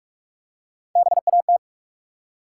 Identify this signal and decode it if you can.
Morse code, heard as a keyed tone, about 44 wpm, 700 Hz. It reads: BAT
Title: Three Letter Words - 45wpm